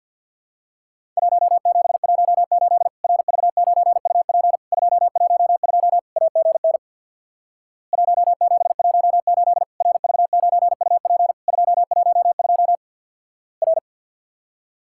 Morse code, heard as a keyed tone, about 50 wpm, 700 Hz. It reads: 1718 RV9UP 212 AGN 1718 RV9UP 212 R
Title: Worked All Europe DX Contest - QTC Exchange with Repeats - 50wpm